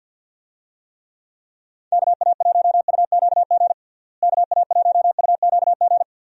Morse code, heard as a keyed tone, about 50 wpm, 700 Hz. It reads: KA1UQG KA1UQG
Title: USA Call Signs - 6 Characters - Repeated 3x - 50wpm